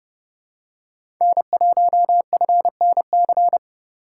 Morse code, heard as a keyed tone, about 32 wpm, 700 Hz. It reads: N1FNC
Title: USA Call Signs - 5 Characters - 2x Character Spacing - 30wpm